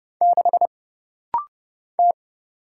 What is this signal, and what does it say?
Morse code, heard as a keyed tone, about 30 wpm, 700 Hz. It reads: 6 T
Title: Single Letter-Number Alt - Rapid-Fire 30wpm